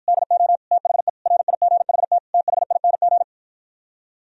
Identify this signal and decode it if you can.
Morse code, heard as a keyed tone, about 52 wpm, 700 Hz. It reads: DO THE RIGHT THING